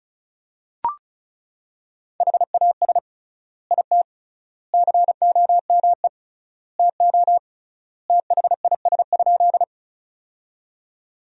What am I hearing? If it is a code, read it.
Morse code, heard as a keyed tone, about 34 wpm, 700 Hz. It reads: HAS IT COME TO THIS?
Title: 3x Word-Spacing - Sentences from Top 100 Words - 35wpm